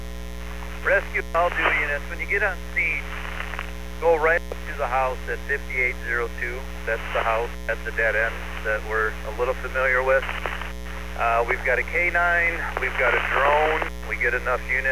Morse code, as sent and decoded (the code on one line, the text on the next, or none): none